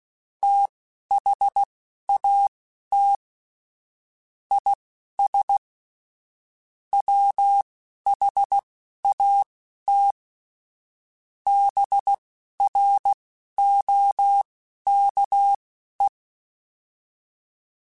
Morse code, heard as a keyed tone, about 16 wpm, spaced as for 12 wpm, 800 Hz. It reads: THAT IS WHAT BROKE
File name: Misc / Misc-mp3s / BIG @10